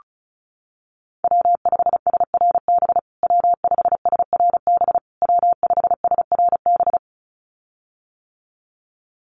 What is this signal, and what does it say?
Morse code, heard as a keyed tone, about 34 wpm, 700 Hz. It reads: W5SRB W5SRB W5SRB
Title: USA Call Signs Repeated 3x 35wpm